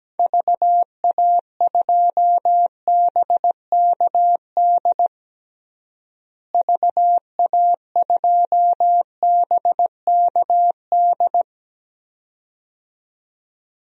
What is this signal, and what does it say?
Morse code, heard as a keyed tone, about 17 wpm, 700 Hz. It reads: VA2BKD VA2BKD
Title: DXCC-100 Call Signs - 6 Characters - Repeated 2x - 17wpm